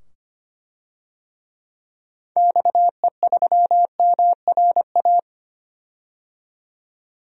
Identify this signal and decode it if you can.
Morse code, heard as a keyed tone, about 25 wpm, 690 Hz. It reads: XE3MRA
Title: DXCC-100 Call Signs - 6 Characters - 25wpm